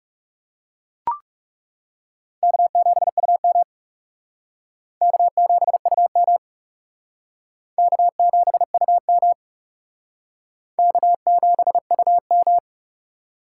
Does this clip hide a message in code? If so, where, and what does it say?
Yes: Morse code from 2.4 to 12.6 s, heard as a keyed tone, about 38 wpm, 700 Hz. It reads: K7UM K7UM K7UM K7UM